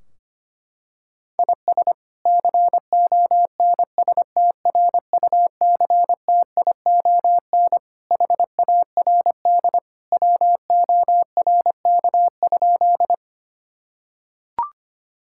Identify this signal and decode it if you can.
Morse code, heard as a keyed tone, about 25 wpm, 700 Hz. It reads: IS CONSTRUCTION HARD WORK?
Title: Sentences from Top 1600 Words 25wpm